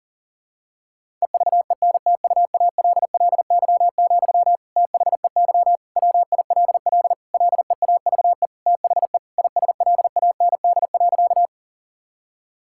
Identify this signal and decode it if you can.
Morse code, heard as a keyed tone, about 40 wpm, 700 Hz. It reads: EVENTUALLY, THEY WILL LEAVE THE ISLAND.
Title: Sentences from Top 1000 Words - Review Part 6 - 40wpm